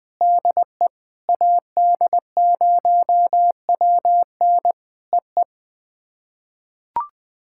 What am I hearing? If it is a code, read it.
Morse code, heard as a keyed tone, about 20 wpm, 700 Hz. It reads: DE AD0WN EE
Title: POTA QSO Practice - 20wpm